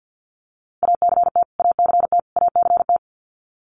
Morse code, heard as a keyed tone, about 50 wpm, 700 Hz. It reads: ACT ACT ACT
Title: Top 500 Words Repeated 50wpm